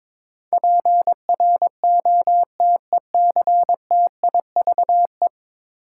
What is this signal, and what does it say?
Morse code, heard as a keyed tone, about 22 wpm, 700 Hz. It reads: PROTECTIVE
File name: practice / 22wpm / 10 Letter Words - First 3 Letters Given - 22wpm